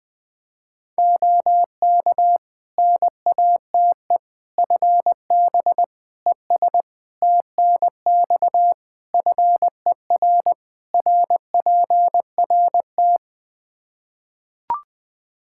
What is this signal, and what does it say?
Morse code, heard as a keyed tone, about 20 wpm, 700 Hz. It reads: OK NATE FB ES TNX FER RPRT
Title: Practice QSO - Exchange 2 - Info - 20wpm